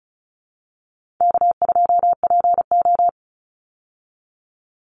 Morse code, heard as a keyed tone, about 34 wpm, 700 Hz. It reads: K2PO
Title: Easy Call Signs Repeated 2x - 35wpm